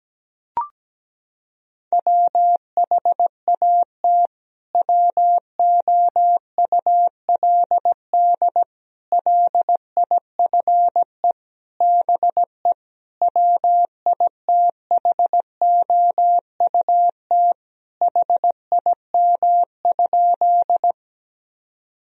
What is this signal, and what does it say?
Morse code, heard as a keyed tone, about 17 wpm, 700 Hz. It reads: WHAT WOULD LIFE BE WITHOUT HIM?
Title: Sentences from Top 1000 Words - Review Part 3 - 17wpm